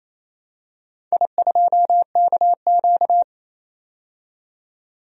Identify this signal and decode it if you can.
Morse code, heard as a keyed tone, about 27 wpm, 700 Hz. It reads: I2KQ
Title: DXCC-100 Call Signs - 4 Characters - 28wpm